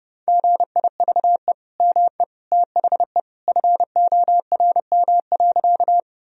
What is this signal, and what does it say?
Morse code, heard as a keyed tone, about 30 wpm, 700 Hz. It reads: GIVE ME THE FORM.
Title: Sentences from Top 1000 Words - Review Part 1 - 30wpm